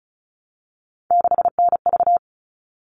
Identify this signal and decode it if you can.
Morse code, heard as a keyed tone, about 34 wpm, 700 Hz. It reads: 6NV
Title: All Cut Numbers - 3-Digit 35wpm